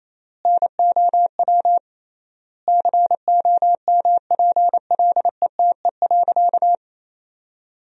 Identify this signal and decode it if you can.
Morse code, heard as a keyed tone, about 28 wpm, 700 Hz. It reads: NOW COMPLETE.
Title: 3x Word-Spacing - Sentences from Top 400 Words - 28wpm